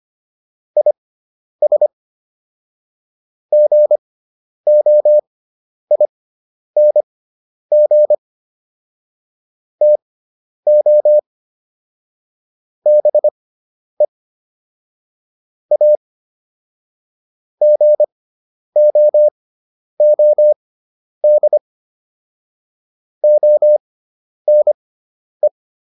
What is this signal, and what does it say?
Morse code, heard as a keyed tone, about 25 wpm, spaced as for 10 wpm, 600 Hz. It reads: IS GOING TO BE A GOOD ONE